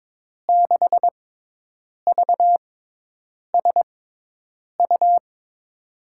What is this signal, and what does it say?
Morse code, heard as a keyed tone, about 22 wpm, spaced as for 8 wpm, 700 Hz. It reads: 6VSU
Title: USA Call Signs - 6 Characters - 6x Character Spacing - 22wpm